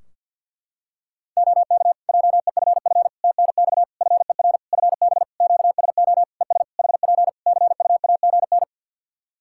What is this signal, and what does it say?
Morse code, heard as a keyed tone, about 50 wpm, 700 Hz. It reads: OK JEFF TNX FER FB QSO ES HP CUAGN
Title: Practice QSO - Exchange 4 - Ending - 50wpm